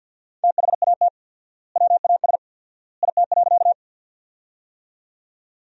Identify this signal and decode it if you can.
Morse code, heard as a keyed tone, about 52 wpm, 700 Hz. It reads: THAT WAS IT.
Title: 4x Word-Spacing - Sentences from Top 100 Words - 50wpm